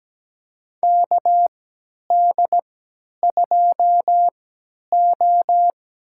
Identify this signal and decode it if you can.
Morse code, heard as a keyed tone, about 17 wpm, spaced as for 10 wpm, 700 Hz. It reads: KD2O
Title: USA Call Signs - 3x Character Spacing - 17wpm